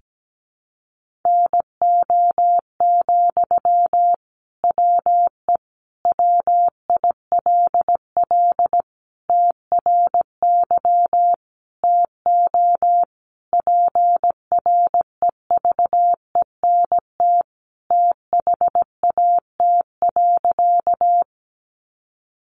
Morse code, heard as a keyed tone, about 17 wpm, 700 Hz. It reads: NO, WE WILL TRY TO PREVENT THAT.
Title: Sentences from Top 1100 Words 17wpm